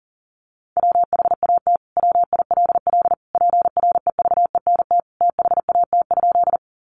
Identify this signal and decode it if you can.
Morse code, heard as a keyed tone, about 40 wpm, 700 Hz. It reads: WHAT WILL PREVENT THAT?
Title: Sentences from Top 1100 Words 40wpm